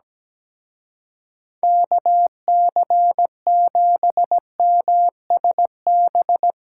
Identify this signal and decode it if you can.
Morse code, heard as a keyed tone, about 18 wpm, 700 Hz. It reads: KC7MSB
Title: Speed Racing - USA 2x3 Call Signs 17wpm